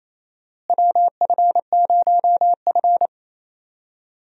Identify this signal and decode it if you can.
Morse code, heard as a keyed tone, about 28 wpm, 700 Hz. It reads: WF0F
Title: USA Call Signs - 4 Characters - Repeated 2x - 28wpm